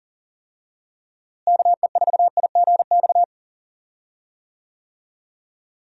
Morse code, heard as a keyed tone, about 40 wpm, 700 Hz. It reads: KE4IGX
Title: USA Call Signs - 6 Characters - Repeated 3x - 40wpm